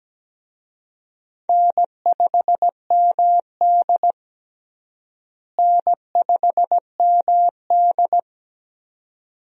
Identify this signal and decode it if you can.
Morse code, heard as a keyed tone, about 17 wpm, 700 Hz. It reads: N5MD N5MD
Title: USA Call Signs - 4 Characters - Repeated 2x - 17wpm